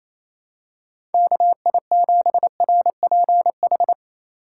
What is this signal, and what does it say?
Morse code, heard as a keyed tone, about 28 wpm, 690 Hz. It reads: KI7RPH
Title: USA Call Signs - 3x Character Spacing - 28wpm